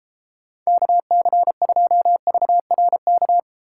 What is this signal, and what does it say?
Morse code, heard as a keyed tone, about 36 wpm, 700 Hz. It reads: KC2VRK